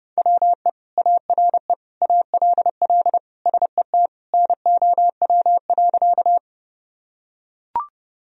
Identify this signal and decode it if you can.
Morse code, heard as a keyed tone, about 30 wpm, 700 Hz. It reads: WE ARE ALL SET NOW.
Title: Sentences from Top 1000 Words - Review Part 1 - 30wpm